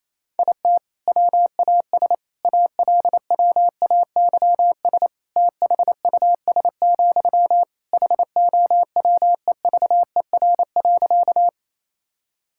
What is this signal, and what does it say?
Morse code, heard as a keyed tone, about 28 wpm, 700 Hz. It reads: IT WAS ALWAYS THUS, HOWEVER.
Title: Sentences from Top 1000 Words - Review Part 5 - 28wpm